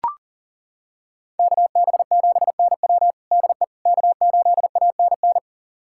Morse code, heard as a keyed tone, about 40 wpm, 700 Hz. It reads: KB7NW DE K8ANN
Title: USA Callsign DE Callsign - 5 Characters - 40wpm